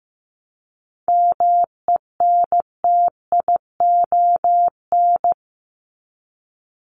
Morse code, heard as a keyed tone, about 15 wpm, 700 Hz. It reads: MENTION
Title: Top 1501-1750 Words 15wpm